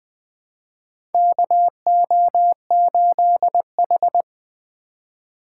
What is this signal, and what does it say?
Morse code, heard as a keyed tone, about 20 wpm, 700 Hz. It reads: KO8H
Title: USA Call Signs - 4 Characters - 4x Character Spacing - 20wpm